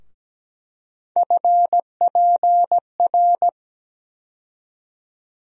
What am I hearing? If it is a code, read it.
Morse code, heard as a keyed tone, about 17 wpm, 700 Hz. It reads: FPR
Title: USA Call Sign Postfixes - 3 Letters 17wpm